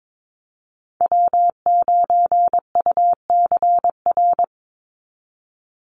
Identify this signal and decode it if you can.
Morse code, heard as a keyed tone, about 22 wpm, 700 Hz. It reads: W9UCR
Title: Call Signs - 22wpm